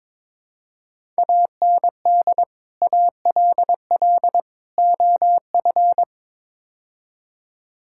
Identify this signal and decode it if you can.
Morse code, heard as a keyed tone, about 22 wpm, 700 Hz. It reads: AND ALL OF
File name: practice / 22wpm / Build A Phrase - 5-Word Phrases - 22wpm